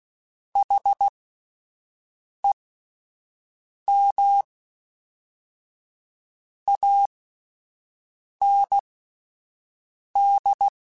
Morse code, heard as a keyed tone, about 16 wpm, spaced as for 5 wpm, 800 Hz. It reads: HEM AND
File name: Misc / Misc-mp3s / Word Pairs @5